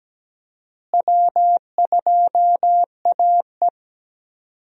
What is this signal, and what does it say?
Morse code, heard as a keyed tone, about 17 wpm, 700 Hz. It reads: W2AE
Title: USA Call Signs - 4 Characters - Repeated 2x - 17wpm